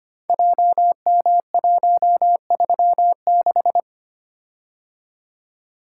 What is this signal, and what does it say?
Morse code, heard as a keyed tone, about 25 wpm, 700 Hz. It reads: JM136
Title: Code Groups - 5 Letter:Numbers - 25wpm